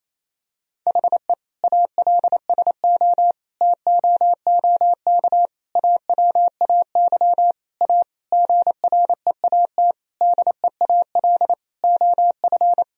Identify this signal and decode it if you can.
Morse code, heard as a keyed tone, about 28 wpm, 700 Hz. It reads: HE ALSO TOOK AWAY A GREAT DEAL OF
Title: Sentences from Top 1000 Words - Review Part 1 - 28wpm